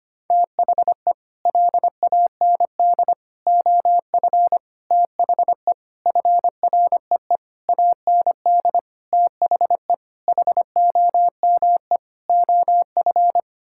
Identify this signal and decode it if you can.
Morse code, heard as a keyed tone, about 25 wpm, 700 Hz. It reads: THE LAND OF THE FREE AND THE HOME OF
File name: Sets of 9 Words - 25wpm